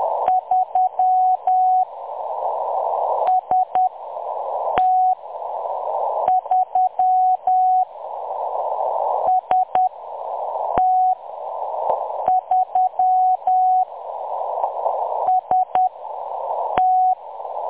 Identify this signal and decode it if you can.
Morse code, heard as a keyed tone, about 10 wpm, spaced as for 6 wpm, 740 Hz. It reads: 3 ST3 ST3 ST
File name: M23-ST3- -0858utc-10184khz